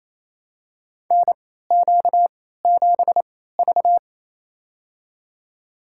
Morse code, heard as a keyed tone, about 28 wpm, spaced as for 16 wpm, 700 Hz. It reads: NQ7V
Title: USA Call Signs - 4 Characters - 3x Character Spacing - 28wpm